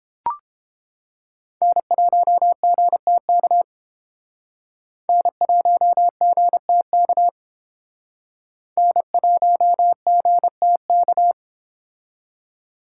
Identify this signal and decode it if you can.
Morse code, heard as a keyed tone, about 30 wpm, 700 Hz. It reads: N1GTK N1GTK N1GTK